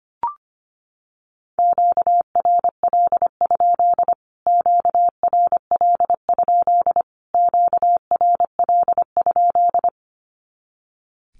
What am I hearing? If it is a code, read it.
Morse code, heard as a keyed tone, about 25 wpm, 700 Hz. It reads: QRL? QRL? QRL?
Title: Q-Codes Repeated 3x 25wpm